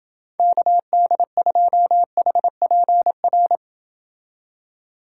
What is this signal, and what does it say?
Morse code, heard as a keyed tone, about 28 wpm, 700 Hz. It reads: KD2HPR